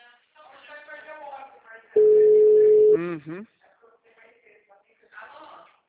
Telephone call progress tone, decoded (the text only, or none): ringback tone